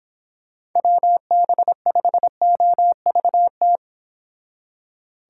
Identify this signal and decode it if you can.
Morse code, heard as a keyed tone, about 26 wpm, 700 Hz. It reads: WB5OVT